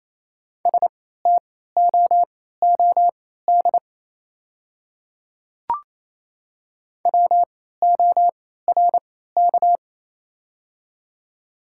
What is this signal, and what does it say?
Morse code, heard as a keyed tone, about 28 wpm, spaced as for 16 wpm, 700 Hz. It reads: STOOD WORK
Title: 3x Character Spacing - Top 500 Words - 28wpm